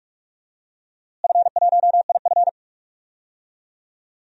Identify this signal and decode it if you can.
Morse code, heard as a keyed tone, about 44 wpm, 700 Hz. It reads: F1IF
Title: DXCC-100 Call Signs - 4 Characters - Repeated 3x - 45wpm